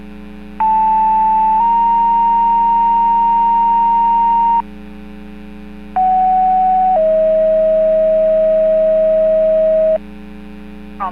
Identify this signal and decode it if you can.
Two-tone page, heard as a one-second tone, then a three-second tone, about 880 then 930 Hz, about 750 then 640 Hz. It is heard from 0.6 to 10.0 s.